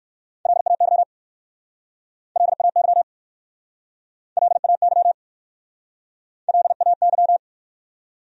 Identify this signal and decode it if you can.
Morse code, heard as a keyed tone, about 59 wpm, 700 Hz. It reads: LAY LAY LAY LAY